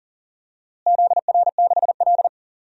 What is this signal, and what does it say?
Morse code, heard as a keyed tone, about 40 wpm, 700 Hz. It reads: ZR6L
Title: DXCC-100 Call Signs - 4 Characters - Repeated 2x - 40wpm